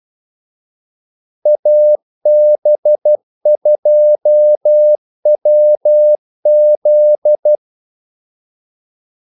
Morse code, heard as a keyed tone, about 12 wpm, 600 Hz. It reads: AB2WZ
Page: 2 ekstra ordmellemrum mellem kaldesignalerne